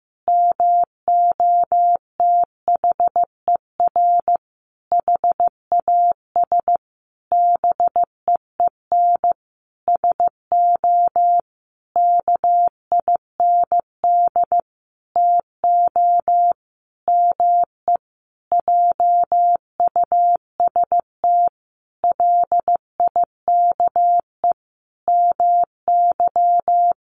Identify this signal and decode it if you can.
Morse code, heard as a keyed tone, about 15 wpm, 700 Hz. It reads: MOTHER HAS BEEN SO KIND TO ME JUST LIKE MY
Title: Sentences from Top 1200 Words 15wpm